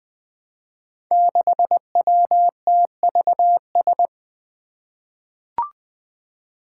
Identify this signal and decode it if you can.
Morse code, heard as a keyed tone, about 20 wpm, 700 Hz. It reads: 6WTVS
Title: Five Challenging Letter-Number Combinations 20wpm